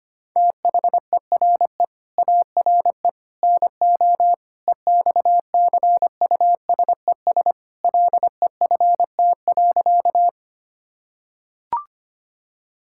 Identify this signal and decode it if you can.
Morse code, heard as a keyed tone, about 25 wpm, 700 Hz. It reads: THERE ARE NO EXCUSES LEFT.
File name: practice / 25wpm / Sentences with 3-Word Phrases - 25wpm